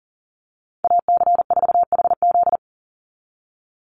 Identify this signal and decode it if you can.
Morse code, heard as a keyed tone, about 40 wpm, 690 Hz. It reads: AC4HZ
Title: USA Call Signs 40wpm